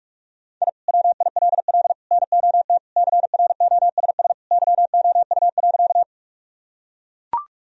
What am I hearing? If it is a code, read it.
Morse code, heard as a keyed tone, about 41 wpm, 700 Hz. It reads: I WILL NOT CROSS YOU.
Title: Sentences from Top 1000 Words - Review Part 1 - 45wpm